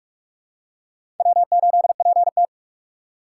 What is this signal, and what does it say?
Morse code, heard as a keyed tone, about 42 wpm, 690 Hz. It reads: W8PT
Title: USA Call Signs - 4x Character Spacing - 45wpm